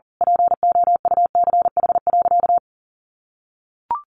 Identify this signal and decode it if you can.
Morse code, heard as a keyed tone, about 40 wpm, 700 Hz. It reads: POUCH.
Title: Easy Phrases 40wpm